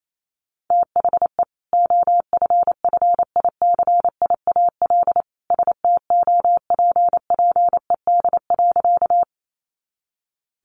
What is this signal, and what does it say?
Morse code, heard as a keyed tone, about 28 wpm, 700 Hz. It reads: THE OFFICIAL STOPPED.
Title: A Sound Of Thunder 28wpm